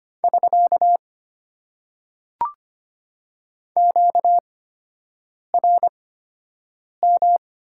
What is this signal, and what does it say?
Morse code, heard as a keyed tone, about 25 wpm, spaced as for 7 wpm, 690 Hz. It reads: <SK> QRM
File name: Top 90 QSO Elements - 8x Character Spacing - 25wpm